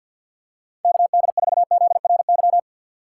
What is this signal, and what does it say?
Morse code, heard as a keyed tone, about 50 wpm, 700 Hz. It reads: KD4ZRY
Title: USA Call Signs - 6 Characters - 6x Character Spacing - 50wpm